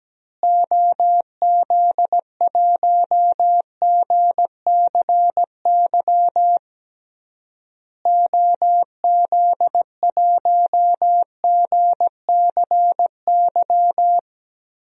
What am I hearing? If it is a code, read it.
Morse code, heard as a keyed tone, about 17 wpm, 700 Hz. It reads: OZ1GCY OZ1GCY
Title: DXCC-100 Call Signs - 6 Characters - Repeated 3x - 17wpm